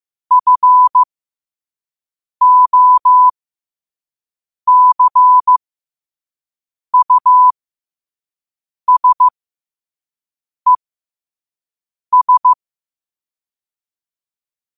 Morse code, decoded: FOCUSES

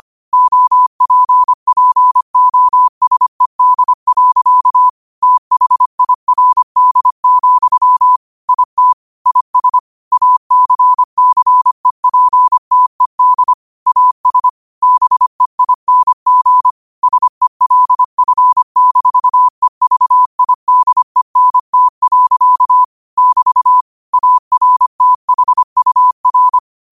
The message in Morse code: OPPOSED. THIRD, IT IS ACCEPTED AS BEING SELF-EVIDENT. <BT> ARTHUR

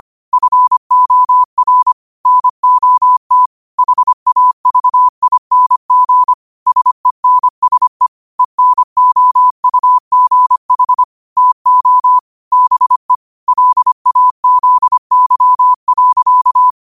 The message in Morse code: FOR NOT HAVING SENSE ENOUGH TO BE LAZY.

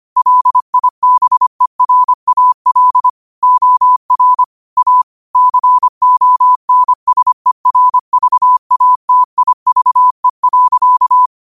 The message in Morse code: LIBERAL OR A CONSERVATIVE.

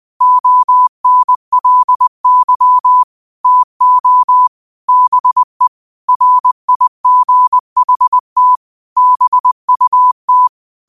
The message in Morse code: ONLY TO BE RIGHT BUT

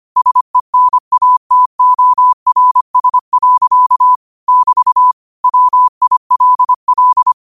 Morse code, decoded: SENATORS. <BT> WILL

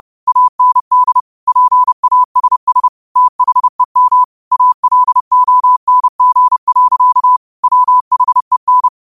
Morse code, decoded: AND PASS THEM ALONG. WHEN

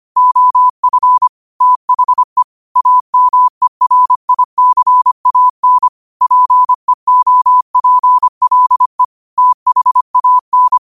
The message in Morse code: OF THE AMERICAN PEOPLE THAN

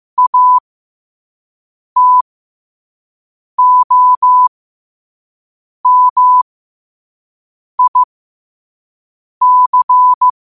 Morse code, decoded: ATOMIC

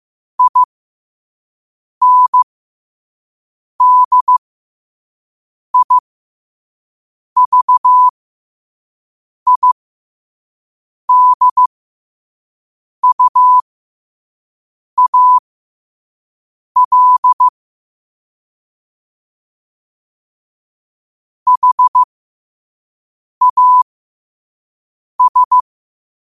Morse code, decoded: INDIVIDUAL HAS

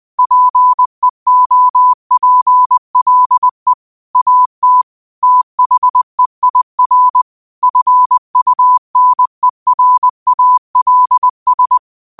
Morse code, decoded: PEOPLE AT THEIR FUNERALS

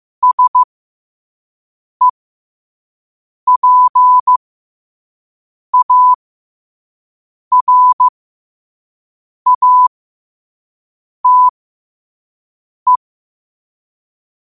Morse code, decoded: SEPARATE

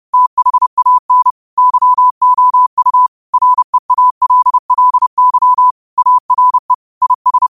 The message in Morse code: THAN YOU REALLY ARE IS

